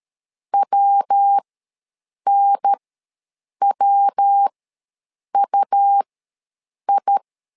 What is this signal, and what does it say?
Morse code, heard as a keyed tone, about 13 wpm, spaced as for 7 wpm, 800 Hz. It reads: WNWUI
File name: LEZIONI / 3 MWRO / 3 MW ws 12wpm 1816